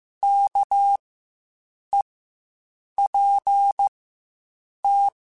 Morse code, heard as a keyed tone, about 15 wpm, spaced as for 7 wpm, 800 Hz. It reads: KEPT
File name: Quotes Spaced / Quotes Spaced-mp3s / Character sp @10